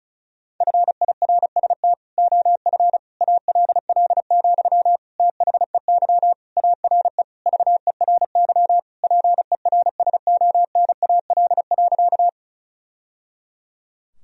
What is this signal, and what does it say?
Morse code, heard as a keyed tone, about 34 wpm, 700 Hz. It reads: FIRST OF ALL, THEY ARE VERY PERSONAL.
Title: Sentences from Top 1000 Words - Review Part 4 - 35wpm